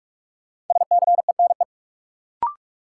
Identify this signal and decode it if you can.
Morse code, heard as a keyed tone, about 42 wpm, 700 Hz. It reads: SCENE
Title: Sentences from Top 1100 Words - New Word Drill - 45wpm